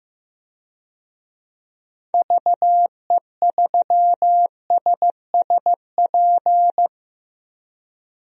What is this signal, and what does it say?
Morse code, heard as a keyed tone, about 15 wpm, 700 Hz. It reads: VE3SSP
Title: DXCC-100 Call Signs - 6 Characters - Repeated 3x - 15wpm